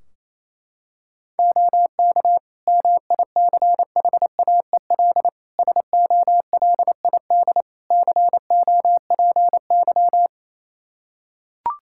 Morse code, heard as a keyed tone, about 27 wpm, 700 Hz. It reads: OK MICHAEL SOLID COPY
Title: Practice QSO - Exchange 3 - More Info - 28wpm